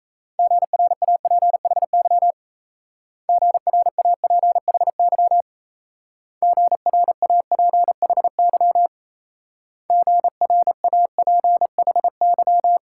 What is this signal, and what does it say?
Morse code, heard as a keyed tone, about 35 wpm, 700 Hz. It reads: GRAPHY GRAPHY GRAPHY GRAPHY